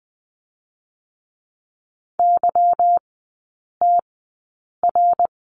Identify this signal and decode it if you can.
Morse code, heard as a keyed tone, about 20 wpm, 690 Hz. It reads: Y T R
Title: 145 Lesson 24e - Letter y - Three Characters 20wpm